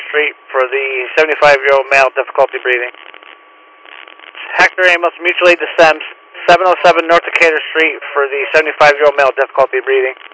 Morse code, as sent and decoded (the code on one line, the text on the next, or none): none